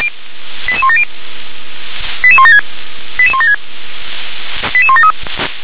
Five-tone selective call (ZVEI), heard as quick runs of five tones, tones of about 70 ms. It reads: 88170 88176 88176 88173
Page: SDR/ZVEI